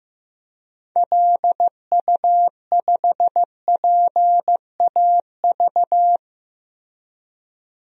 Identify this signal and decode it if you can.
Morse code, heard as a keyed tone, about 15 wpm, 700 Hz. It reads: LU5PAV
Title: DXCC-100 Call Signs - 6 Characters - Repeated 2x - 15wpm